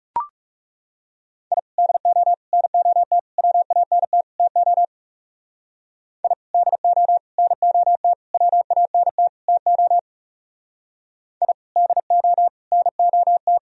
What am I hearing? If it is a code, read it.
Morse code, heard as a keyed tone, about 34 wpm, 700 Hz. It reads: I DO NOT WANT TO I DO NOT WANT TO I DO NOT
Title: Speed-Racing - Sets of 5 Words 30wpm